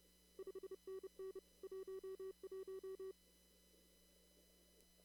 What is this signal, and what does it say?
Morse code, heard as a keyed tone, about 30 wpm, 380 Hz. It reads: 5NN 11
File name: CW / kt5j wpx cw-op-k5pi